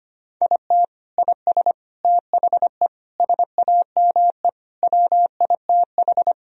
Morse code, heard as a keyed tone, about 25 wpm, 700 Hz. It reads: IT IS THE SAME WITH
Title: Sentences from Top 1000 Words - Review Part 3 - 25wpm